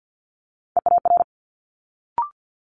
Morse code, heard as a keyed tone, about 50 wpm, 700 Hz. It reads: EAR
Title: Multi-letter Phonemes Quick Fire 50wpm